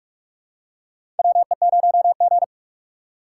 Morse code, heard as a keyed tone, about 46 wpm, 700 Hz. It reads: WE0G
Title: USA Call Signs - 4x Character Spacing - 45wpm